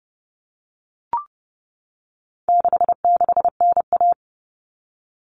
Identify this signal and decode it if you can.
Morse code, heard as a keyed tone, about 30 wpm, 700 Hz. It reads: 66NA
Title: All Cut Numbers - 4-Digit 30wpm